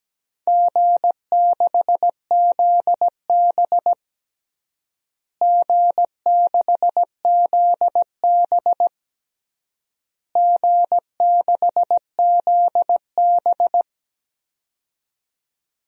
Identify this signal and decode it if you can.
Morse code, heard as a keyed tone, about 17 wpm, 700 Hz. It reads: G6ZB G6ZB G6ZB
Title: DXCC-100 Call Signs - 4 Characters - Repeated 3x - 17wpm